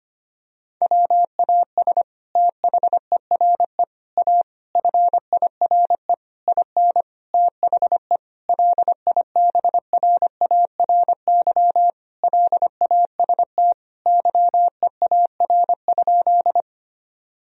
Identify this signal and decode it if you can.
Morse code, heard as a keyed tone, about 25 wpm, 700 Hz. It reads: WAS THERE A FIRE IN THE LIBRARY LAST YEAR?
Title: Sentences from Top 1000 Words - Review Part 4 - 25wpm